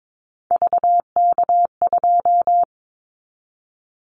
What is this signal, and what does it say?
Morse code, heard as a keyed tone, about 22 wpm, 700 Hz. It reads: VK2
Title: Callsign Prefixes for Top 100 DXCC - 22wpm